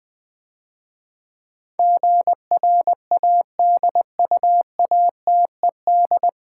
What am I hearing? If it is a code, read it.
Morse code, heard as a keyed tone, about 20 wpm, 700 Hz. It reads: GRADUATED
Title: Easy Words - Speed-Racing with Hints - 20wpm